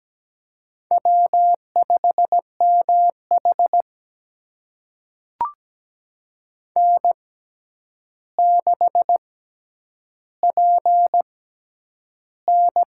Morse code, heard as a keyed tone, about 17 wpm, spaced as for 9 wpm, 700 Hz. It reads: W5MH N6PN
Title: USA Call Signs - 4 Characters - 6x Character Spacing - 17wpm